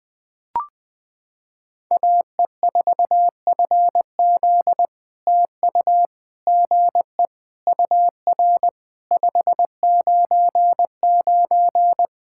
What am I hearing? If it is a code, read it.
Morse code, heard as a keyed tone, about 20 wpm, 700 Hz. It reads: AE4FZ TU GE UR 599
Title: POTA QSO Practice - 20wpm